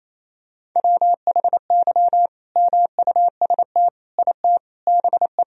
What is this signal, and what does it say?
Morse code, heard as a keyed tone, about 28 wpm, 700 Hz. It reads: WHY MUST IT BE
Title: Sentences from Top 1000 Words - Review Part 3 - 28wpm